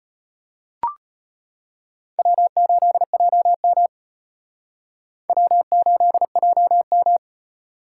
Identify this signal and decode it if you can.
Morse code, heard as a keyed tone, about 31 wpm, 700 Hz. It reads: W8JM W8JM
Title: Speed Racing - USA 1x2 Call Signs 25wpm